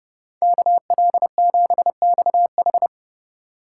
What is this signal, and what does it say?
Morse code, heard as a keyed tone, about 36 wpm, 700 Hz. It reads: KL7XH
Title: Speed Racing - USA 2x2 Call Signs 30wpm